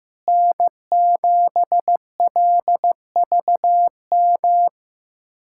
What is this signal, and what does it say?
Morse code, heard as a keyed tone, about 15 wpm, 700 Hz. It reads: N7LVM
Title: USA Callsign DE Callsign - 5 Characters - 15wpm